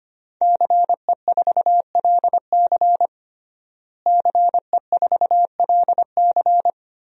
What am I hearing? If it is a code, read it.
Morse code, heard as a keyed tone, about 25 wpm, 700 Hz. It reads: CE4LC CE4LC
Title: DXCC-100 Call Signs - 5 Characters - Repeated 2x - 25wpm